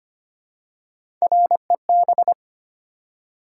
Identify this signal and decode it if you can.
Morse code, heard as a keyed tone, about 25 wpm, 700 Hz. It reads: REB
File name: Three Letter-Number Combinations - 25wpm